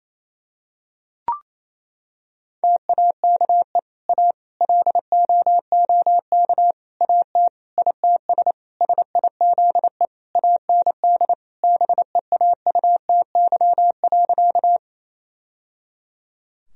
Morse code, heard as a keyed tone, about 28 wpm, 700 Hz. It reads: TAKE A LOOK AT ITS SIZE AND BEAUTY.
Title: Sentences from Top 1000 Words - Review Part 2 - 28wpm